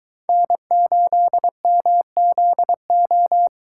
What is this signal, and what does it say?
Morse code, heard as a keyed tone, about 22 wpm, 700 Hz. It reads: N8MZO